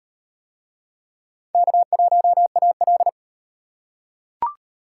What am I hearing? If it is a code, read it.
Morse code, heard as a keyed tone, about 37 wpm, 700 Hz. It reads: K1AL